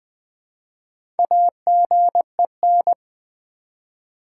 Morse code, heard as a keyed tone, about 20 wpm, 700 Hz. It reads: AGEN
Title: Build A Word - 1 Letter At A Time - 3 to 5-Letter Words - 20wpm